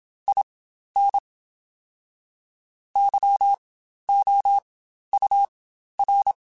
Morse code, heard as a keyed tone, about 27 wpm, spaced as for 12 wpm, 800 Hz. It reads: IN YOUR
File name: Quotes Spaced / Quotes Spaced-mp3s / ChildAsleep sp @25